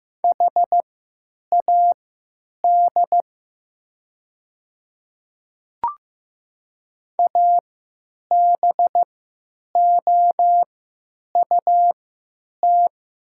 Morse code, heard as a keyed tone, about 15 wpm, spaced as for 9 wpm, 690 Hz. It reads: HAD ABOUT
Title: 3x Character Spacing - Top 100 Words - 15wpm